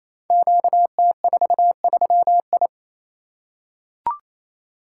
Morse code, heard as a keyed tone, about 27 wpm, 700 Hz. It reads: QT43I
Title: Code Groups - 5 Letter:Numbers - 28wpm